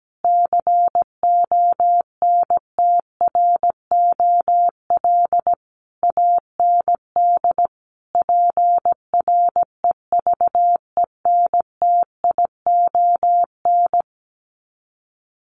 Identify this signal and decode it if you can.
Morse code, heard as a keyed tone, about 17 wpm, 700 Hz. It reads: CONTROL AND PREVENTION
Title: Sets of 5 Words Encore2 17wpm